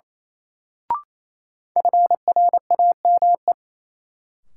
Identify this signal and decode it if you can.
Morse code, heard as a keyed tone, about 28 wpm, 700 Hz. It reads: FRAME
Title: Sentences from Top 1000 Words - Review Part 6 - New Word Drill - 28wpm